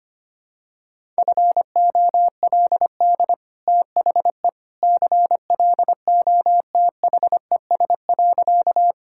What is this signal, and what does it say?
Morse code, heard as a keyed tone, about 25 wpm, 690 Hz. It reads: FOLD THE CLOTHES.
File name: Easy Sentences with Hints - 25wpm